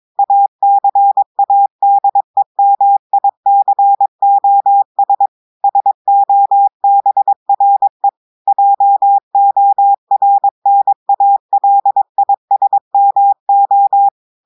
Morse code, heard as a keyed tone, about 22 wpm, 800 Hz. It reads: ACADEMICOS SOBRE JORNALISMO
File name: audios / portugais / portugais texte en clair 1 vitesse 22 mots minutes